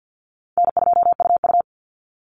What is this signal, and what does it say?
Morse code, heard as a keyed tone, about 50 wpm, 700 Hz. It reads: N2UU
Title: Easy Call Signs Repeated 3x - 50wpm